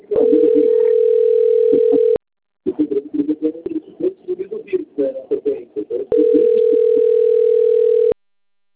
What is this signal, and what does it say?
Ringback tone (North America), 440 + 480 Hz.